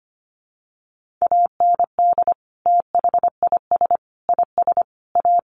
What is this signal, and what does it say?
Morse code, heard as a keyed tone, about 25 wpm, 700 Hz. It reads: AND THIS IS A
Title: Sentences from Top 1400 Words 25wpm